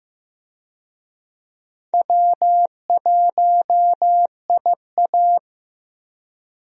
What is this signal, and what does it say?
Morse code, heard as a keyed tone, about 15 wpm, 700 Hz. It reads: W1IA
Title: USA Call Signs - 4 Characters - 15wpm